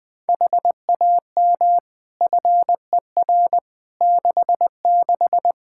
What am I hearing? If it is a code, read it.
Morse code, heard as a keyed tone, about 21 wpm, 700 Hz. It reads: HAM FER 66
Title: Practice QSO - Exchange 3 - More Info - 20wpm